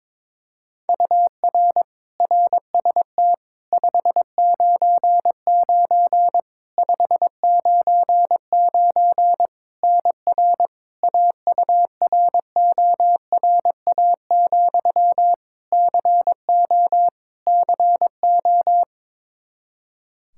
Morse code, heard as a keyed tone, about 22 wpm, 700 Hz. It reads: UR RST 599 599 NR AURORA, CO CO
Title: K1USN SST QSO Practice 22wpm